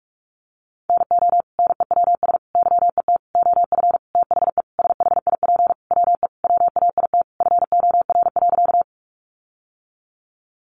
Morse code, heard as a keyed tone, about 44 wpm, 700 Hz. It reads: NO NEWS YET OF THE SHIP WE WAIT FOR.
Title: Sentences from Top 800 Words 45wpm